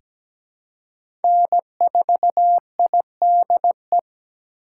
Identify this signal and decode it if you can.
Morse code, heard as a keyed tone, about 17 wpm, 700 Hz. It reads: N4IDE
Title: USA Call Signs - 5 Characters - 4x Character Spacing - 17wpm